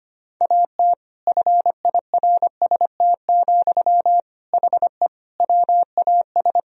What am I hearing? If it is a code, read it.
Morse code, heard as a keyed tone, about 25 wpm, 700 Hz. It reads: AT FIRST, HE WAS
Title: Sentences from Top 1000 Words - Review Part 4 - 25wpm